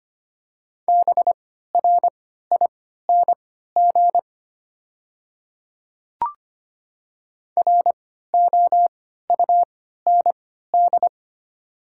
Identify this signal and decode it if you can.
Morse code, heard as a keyed tone, about 25 wpm, spaced as for 14 wpm, 700 Hz. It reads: BRING ROUND
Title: 3x Character Spacing - Top 500 Words - 25wpm